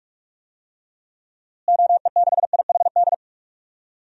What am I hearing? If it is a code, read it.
Morse code, heard as a keyed tone, about 46 wpm, 700 Hz. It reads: OE6IHD
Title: DXCC-100 Call Signs - 6 Characters - Repeated 2x - 45wpm